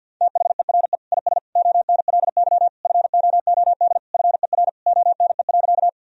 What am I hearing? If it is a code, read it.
Morse code, heard as a keyed tone, about 48 wpm, 700 Hz. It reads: THERE IS ONLY FOOD FER ONE.
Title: Sentences from Top 300 Words and Abbreviations 50wpm